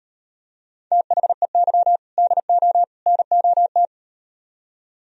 Morse code, heard as a keyed tone, about 38 wpm, 700 Hz. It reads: THEY DO NOT